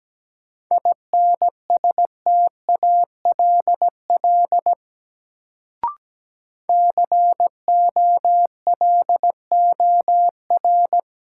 Morse code, heard as a keyed tone, about 17 wpm, 700 Hz. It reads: INSTALL COLOR
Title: Sentences from Top 1600 Words - New Word Drill Alt - 17wpm